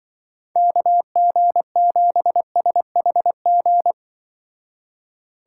Morse code, heard as a keyed tone, about 24 wpm, 700 Hz. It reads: KG7SHG
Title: Speed Racing - USA 2x3 Call Signs 20wpm